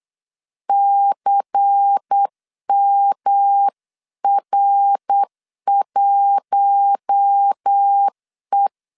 Morse code, heard as a keyed tone, about 8 wpm, 800 Hz. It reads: CMR1E
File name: LEZIONI / 8 YJ45 / 8 YJ4 8wpm 3246